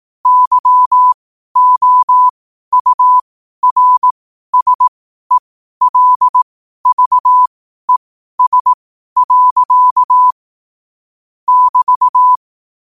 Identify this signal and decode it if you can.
Morse code, heard as a keyed tone, about 18 wpm, spaced as for 12 wpm, 1000 Hz. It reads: YOURSELVES. <BT>
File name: qotdfiles / 12wpm